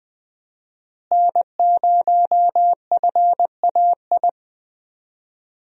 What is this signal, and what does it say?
Morse code, heard as a keyed tone, about 20 wpm, 700 Hz. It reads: N0FAI
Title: USA Call Signs - 5 Characters - Repeated 3x - 20wpm